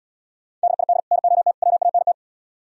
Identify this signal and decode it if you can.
Morse code, heard as a keyed tone, about 77 wpm, 700 Hz. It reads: DID NOT WANT